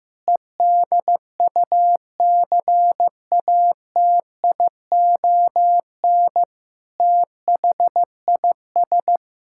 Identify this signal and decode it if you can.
Morse code, heard as a keyed tone, about 16 wpm, 700 Hz. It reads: EDUCATION THIS
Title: Sentences from Top 1000 Words - Review Part 6 - 15wpm